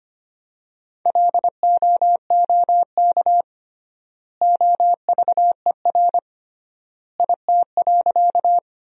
Morse code, heard as a keyed tone, about 25 wpm, 700 Hz. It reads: LOOK OVER IT.
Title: 3x Word-Spacing - Sentences from Top 100 Words - 25wpm